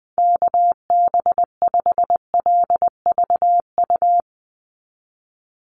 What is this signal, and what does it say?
Morse code, heard as a keyed tone, about 20 wpm, 700 Hz. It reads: KB5LVU
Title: USA Call Signs 20wpm